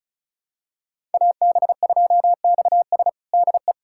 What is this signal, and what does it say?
Morse code, heard as a keyed tone, about 34 wpm, 700 Hz. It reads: AB2XS DE
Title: USA Callsign DE Callsign - 5 Characters - 35wpm